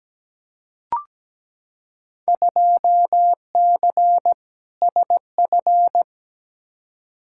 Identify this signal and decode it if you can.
Morse code, heard as a keyed tone, about 17 wpm, 700 Hz. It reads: 2C SF
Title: Field Day Exchanges Encore - 17wpm